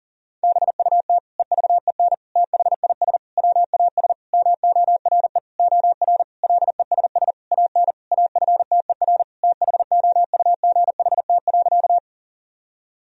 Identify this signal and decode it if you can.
Morse code, heard as a keyed tone, about 40 wpm, 700 Hz. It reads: BUT EVEN THIS WAS MORE OR LESS AN AFTER THOUGHT.
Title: Sentences from Top 1000 Words - Review Part 2 - 40wpm